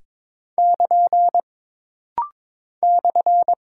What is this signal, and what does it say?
Morse code, heard as a keyed tone, about 22 wpm, 700 Hz. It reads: <KN> /
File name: practice / 22wpm / Single Punctuation-Prosign - Rapid-Fire - 22wpm